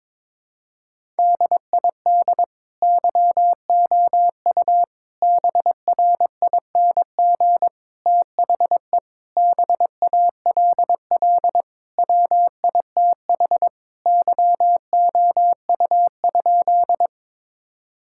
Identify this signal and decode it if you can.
Morse code, heard as a keyed tone, about 22 wpm, 700 Hz. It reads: DID YOU BRING THE BALL WITH YOU?
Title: Sentences from Top 1000 Words - Review Part 6 - 22wpm